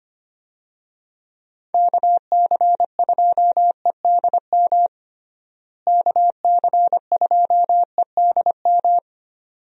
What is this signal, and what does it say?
Morse code, heard as a keyed tone, about 25 wpm, 700 Hz. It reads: KC2EDM KC2EDM
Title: USA Call Signs - 6 Characters - Repeated 2x - 25wpm